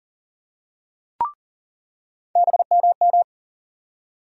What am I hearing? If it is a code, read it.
Morse code, heard as a keyed tone, about 40 wpm, 700 Hz. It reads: BMM
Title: USA Call Sign Postfixes - 3 Letters 40wpm